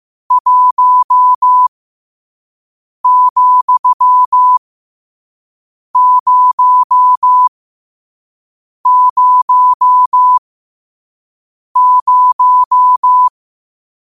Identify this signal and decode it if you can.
Morse code, heard as a keyed tone, about 15 wpm, spaced as for 5 wpm, 1000 Hz. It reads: 1,000
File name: qotdfiles / 5wpm